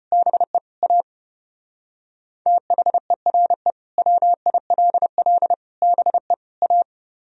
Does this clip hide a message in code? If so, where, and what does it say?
Yes: Morse code from 0.1 to 6.8 s, heard as a keyed tone, about 32 wpm, 700 Hz. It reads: BE A THERE WILL BE A